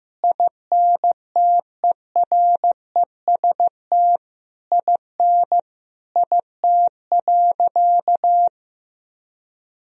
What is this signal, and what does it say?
Morse code, heard as a keyed tone, about 15 wpm, 700 Hz. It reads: INTEREST IN IT.
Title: Sentences from Top 1000 Words - Review Part 2 - 15wpm